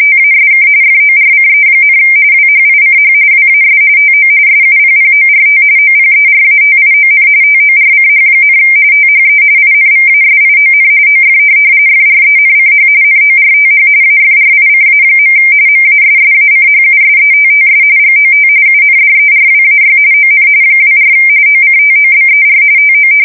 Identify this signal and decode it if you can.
RTTY, about 45.45 baud. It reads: TASK FORCE, WHICH WAS TO SUPPORT OPERATION TORCH, THE INVASION OF FRENCH NORTH AFRICA. MASSACHUSETTS GOT UNDERWAY ON 24 OCTOBER TO